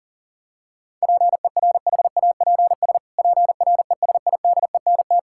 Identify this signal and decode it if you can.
Morse code, heard as a keyed tone, about 40 wpm, 700 Hz. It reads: PERHAPS PRESIDENT
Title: Pangram Challenge v2 - 40wpm